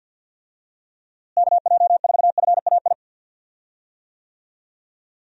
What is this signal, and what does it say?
Morse code, heard as a keyed tone, about 52 wpm, 700 Hz. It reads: KJ4FAI
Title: USA Call Signs - 6 Characters - Repeated 3x - 50wpm